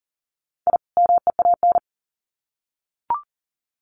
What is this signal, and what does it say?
Morse code, heard as a keyed tone, about 40 wpm, 700 Hz. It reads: I MEAN
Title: Sets of 2 Words 40wpm